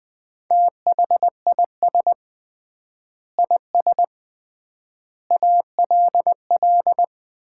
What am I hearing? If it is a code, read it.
Morse code, heard as a keyed tone, about 20 wpm, 700 Hz. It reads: THIS IS ALL
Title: 3x Word-Spacing - Sets of 5 Words - 20wpm